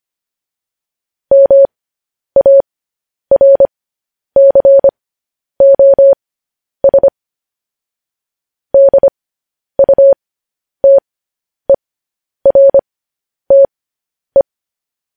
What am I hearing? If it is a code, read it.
Morse code, heard as a keyed tone, about 25 wpm, spaced as for 10 wpm, 550 Hz. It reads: MARCOS DUTERTE
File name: podcast / News headlines-1x-25x10